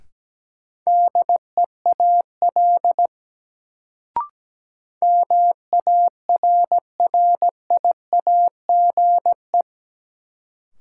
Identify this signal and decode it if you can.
Morse code, heard as a keyed tone, about 17 wpm, 700 Hz. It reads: DEAL MARRIAGE
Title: Sentences from Top 800 Words - New Word Drill - 17wpm